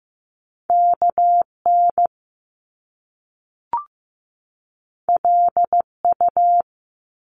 Morse code, heard as a keyed tone, about 15 wpm, 700 Hz. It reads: KN LU
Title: Top Two Letter Combinations in Words 15wpm